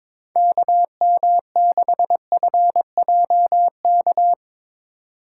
Morse code, heard as a keyed tone, about 21 wpm, 700 Hz. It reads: KM6FJK
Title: USA Call Signs - 6 Characters - 2x Character Spacing - 22wpm